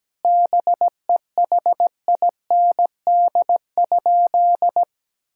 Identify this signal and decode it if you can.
Morse code, heard as a keyed tone, about 17 wpm, 700 Hz. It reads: BEHIND?
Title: Sentences from Top 1000 Words - Review Part 5 - 17wpm